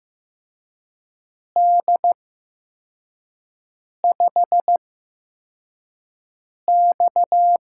Morse code, heard as a keyed tone, about 15 wpm, spaced as for 4 wpm, 700 Hz. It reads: D5X
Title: USA Call Signs - 4 Characters - 8x Character Spacing - 15wpm